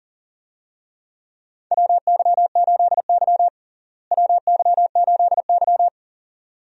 Morse code, heard as a keyed tone, about 37 wpm, 700 Hz. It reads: WY8Y WY8Y
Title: USA Call Signs - 4 Characters - Repeated 2x - 40wpm